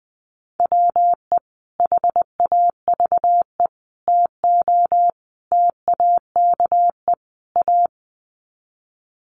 Morse code, heard as a keyed tone, about 20 wpm, 700 Hz. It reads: WE HAVE TO TAKE A
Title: Sets of 5 Words Encore 20wpm